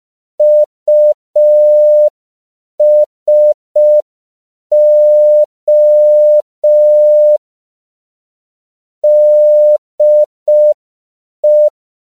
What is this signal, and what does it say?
Morse code, heard as a keyed tone, about 5 wpm, 600 Hz. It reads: USO DE
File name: CW / 5PPM